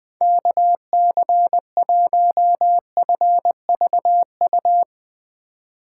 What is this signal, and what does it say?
Morse code, heard as a keyed tone, about 20 wpm, 700 Hz. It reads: KC1FVU
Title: Speed Racing - USA 2x3 Call Signs 17wpm